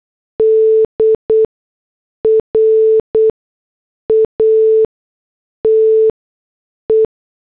Morse code, heard as a keyed tone, about 8 wpm, spaced as for 6 wpm, 440 Hz. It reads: D RATE